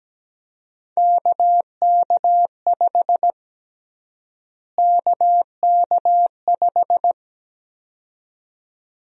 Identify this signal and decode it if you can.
Morse code, heard as a keyed tone, about 17 wpm, 700 Hz. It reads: KK5 KK5
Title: USA Call Sign Prefixes - Repeated 2x - 17wpm